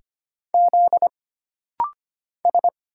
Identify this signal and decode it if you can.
Morse code, heard as a keyed tone, about 25 wpm, 700 Hz. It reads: Z S
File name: Single Character (Letter:Number:Punctuation:Prosign) - Rapid-Fire - 25wpm